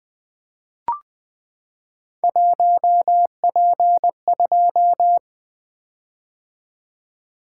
Morse code, heard as a keyed tone, about 20 wpm, 700 Hz. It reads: 1P2
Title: Three Letter-Number Combinations - 20wpm